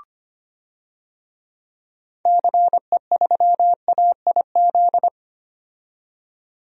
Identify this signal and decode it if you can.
Morse code, heard as a keyed tone, about 25 wpm, 700 Hz. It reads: CE3AIZ
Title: DXCC-100 Call Signs - 6 Characters - 25wpm